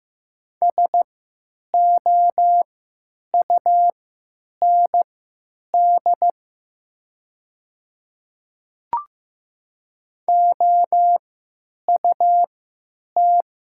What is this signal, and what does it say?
Morse code, heard as a keyed tone, about 15 wpm, spaced as for 9 wpm, 700 Hz. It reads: SOUND OUT
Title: 3x Character Spacing - Top 100 Words - 15wpm